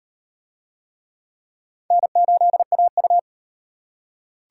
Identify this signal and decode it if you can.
Morse code, heard as a keyed tone, about 38 wpm, 700 Hz. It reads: N8AU